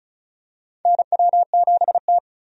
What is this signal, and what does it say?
Morse code, heard as a keyed tone, about 36 wpm, 700 Hz. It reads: NW7T
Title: USA Call Signs - 4 Characters - 35wpm